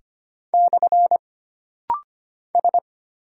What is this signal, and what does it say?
Morse code, heard as a keyed tone, about 25 wpm, 700 Hz. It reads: / S
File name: Single Character (Letter:Number:Punctuation:Prosign) - Rapid-Fire - 25wpm